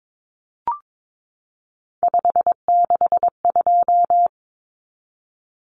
Morse code, heard as a keyed tone, about 22 wpm, 700 Hz. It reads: 562
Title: Common Cut Numbers - 3-Digit 22wpm